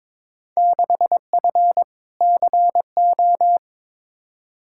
Morse code, heard as a keyed tone, about 21 wpm, 700 Hz. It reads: 6F CO
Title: Field Day Exchanges Encore - 22wpm